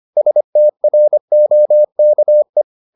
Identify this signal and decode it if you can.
Morse code, heard as a keyed tone, about 25 wpm, 600 Hz. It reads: STROKE